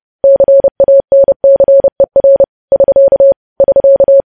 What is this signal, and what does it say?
Morse code, heard as a keyed tone, about 30 wpm, 550 Hz. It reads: CANCER <SK> <SK>
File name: podcast / News headlines-1x-30x30